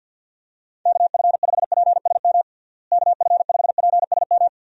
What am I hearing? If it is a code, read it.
Morse code, heard as a keyed tone, about 48 wpm, 700 Hz. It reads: KF5PSM KF5PSM
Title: USA Call Signs - 6 Characters - Repeated 2x - 50wpm